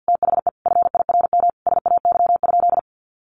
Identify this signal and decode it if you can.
Morse code, heard as a keyed tone, about 50 wpm, 700 Hz. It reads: THE FIRM SAY?
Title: Sentences from Top 1400 Words 50wpm